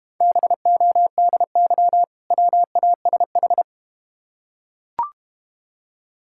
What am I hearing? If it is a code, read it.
Morse code, heard as a keyed tone, about 31 wpm, 690 Hz. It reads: BODY WASH
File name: Easy Words - Speed-Racing with Hints - 25wpm